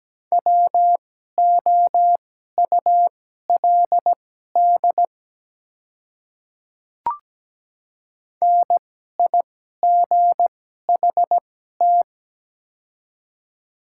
Morse code, heard as a keyed tone, about 17 wpm, spaced as for 12 wpm, 700 Hz. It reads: WOULD NIGHT
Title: 2x Character Spacing - Top 300 Words - 17wpm